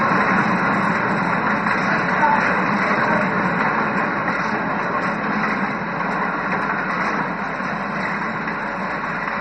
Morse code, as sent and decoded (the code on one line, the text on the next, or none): none